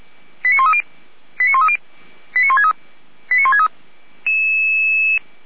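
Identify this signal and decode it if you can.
Five-tone selective call (ZVEI), heard as quick runs of five tones, tones of about 70 ms. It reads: 89130 89130 89163 89163
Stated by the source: ZVEI five-tone selective call received on an SDR.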